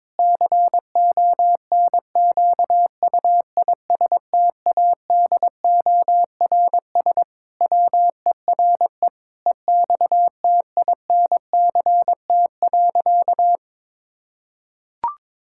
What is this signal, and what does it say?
Morse code, heard as a keyed tone, about 22 wpm, 700 Hz. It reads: CONQUISTADORS WERE EXTINCT.
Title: Pangram Challenge v2 - 22wpm